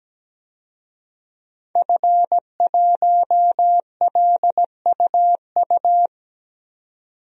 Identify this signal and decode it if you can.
Morse code, heard as a keyed tone, about 17 wpm, 700 Hz. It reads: F1LUU
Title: DXCC-100 Call Signs - 5 Characters - 17wpm